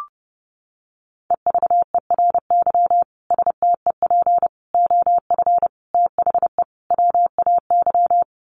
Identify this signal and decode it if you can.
Morse code, heard as a keyed tone, about 30 wpm, 700 Hz. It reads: EVERY STEP OF THE WAY
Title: Sets of 5 Words Encore2 30wpm